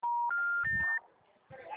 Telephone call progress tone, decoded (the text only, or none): special information tone